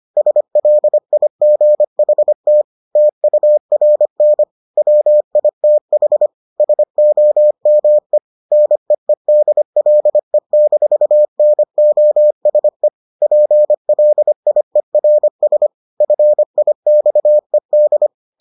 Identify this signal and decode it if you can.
Morse code, heard as a keyed tone, about 25 wpm, 600 Hz. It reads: SLIGHT TURN WITH SOME NEEDLE-NOSE PLIERS FIXED